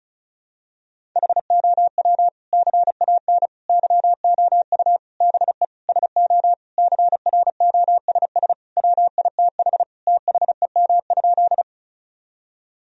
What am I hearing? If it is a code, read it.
Morse code, heard as a keyed tone, about 36 wpm, 700 Hz. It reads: HOW CAN YOU BE SO CROSS WITH THEM?
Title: Sentences from Top 1000 Words - Review Part 1 - 35wpm